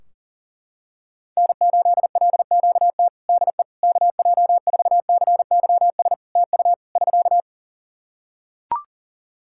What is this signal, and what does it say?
Morse code, heard as a keyed tone, about 37 wpm, 700 Hz. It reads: N8LQT DE KJ4CYS TU <SK>
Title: K1USN SST QSO Practice 40wpm